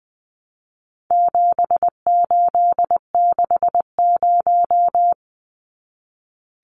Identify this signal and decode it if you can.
Morse code, heard as a keyed tone, about 20 wpm, 700 Hz. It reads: 7860
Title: Numbers - 4-Digit 20wpm